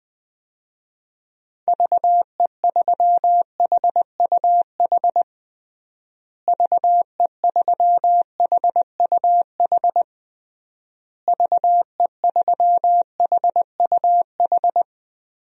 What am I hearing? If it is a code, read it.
Morse code, heard as a keyed tone, about 20 wpm, 700 Hz. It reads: VE3HUH VE3HUH VE3HUH
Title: DXCC-100 Call Signs - 6 Characters - Repeated 3x - 20wpm